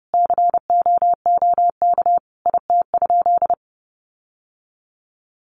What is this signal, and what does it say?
Morse code, heard as a keyed tone, about 30 wpm, 700 Hz. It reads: COOK IT?
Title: Sentences from Top 1500 Words 30wpm